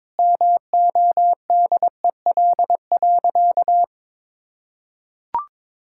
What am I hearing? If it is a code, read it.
Morse code, heard as a keyed tone, about 22 wpm, 700 Hz. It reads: MODEL.
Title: Sentences from Top 1000 Words - Review Part 5 - 22wpm